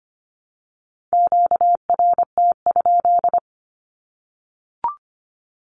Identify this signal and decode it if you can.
Morse code, heard as a keyed tone, about 25 wpm, 700 Hz. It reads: QRT?
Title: Q-Codes Repeated 3x 25wpm